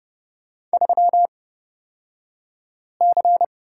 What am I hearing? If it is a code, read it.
Morse code, heard as a keyed tone, about 30 wpm, spaced as for 5 wpm, 700 Hz. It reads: 3C
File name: Lesson 36d - BT - Two Characters 30wpm